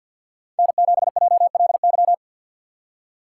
Morse code, heard as a keyed tone, about 53 wpm, 700 Hz. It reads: N7JLY
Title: USA Call Signs - 5 Characters - 6x Character Spacing - 50wpm